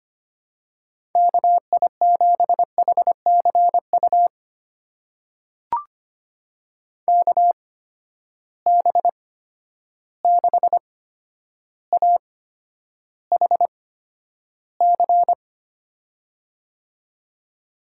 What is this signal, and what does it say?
Morse code, heard as a keyed tone, about 25 wpm, spaced as for 11 wpm, 700 Hz. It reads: KI7HCU KB6AHC
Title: USA Call Signs - 6 Characters - 8x Character Spacing - 25wpm